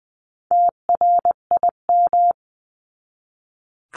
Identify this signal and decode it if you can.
Morse code, heard as a keyed tone, about 20 wpm, 700 Hz. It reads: TRIM
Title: 259 Lesson 42d - Words 18-20wpm